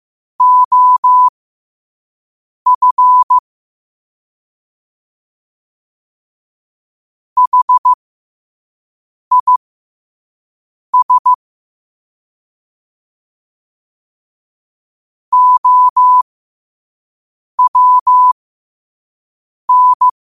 Morse code, heard as a keyed tone, about 15 wpm, spaced as for 5 wpm, 1000 Hz. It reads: OF HIS OWN
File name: qotdfiles / 5wpm